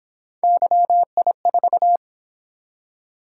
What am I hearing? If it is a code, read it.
Morse code, heard as a keyed tone, about 27 wpm, 700 Hz. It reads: YI4
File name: Speed-Racing - Three Letter-Number Combinations - 22wpm